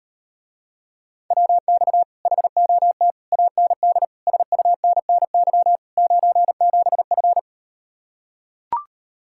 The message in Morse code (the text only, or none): WX HOT AND SUNNY 97F